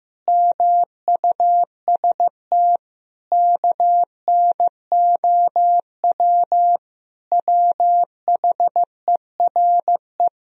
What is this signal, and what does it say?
Morse code, heard as a keyed tone, about 15 wpm, 700 Hz. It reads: MUST KNOW WHERE
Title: Sentences from Top 1000 Words - Review Part 4 - 15wpm